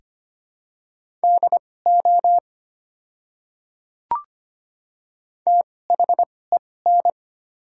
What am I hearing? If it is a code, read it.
Morse code, heard as a keyed tone, about 25 wpm, spaced as for 18 wpm, 690 Hz. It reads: DO THEN
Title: 2x Character Spacing - Top 200 Words - 25wpm